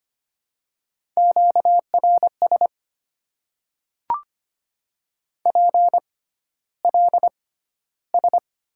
Morse code, heard as a keyed tone, about 25 wpm, spaced as for 13 wpm, 700 Hz. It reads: QRS PLS
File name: Top 90 QSO Elements - 6x Character Spacing - 25wpm